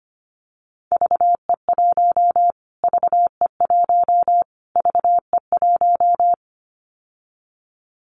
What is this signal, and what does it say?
Morse code, heard as a keyed tone, about 25 wpm, 700 Hz. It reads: VE1 VE1 VE1
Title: Callsign Prefixes for Top 25 DXCC - Repeated 3x - 25wpm